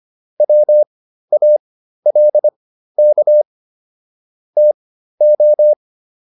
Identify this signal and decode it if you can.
Morse code, heard as a keyed tone, about 25 wpm, spaced as for 13 wpm, 600 Hz. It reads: WALK TO
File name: PR402 13